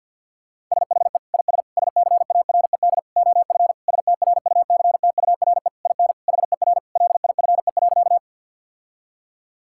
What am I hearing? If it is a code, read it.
Morse code, heard as a keyed tone, about 50 wpm, 700 Hz. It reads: SHE IS SCARED OF STRUCTURE IN HER LIFE.
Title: Sentences from Top 1000 Words - Review Part 6 - 50wpm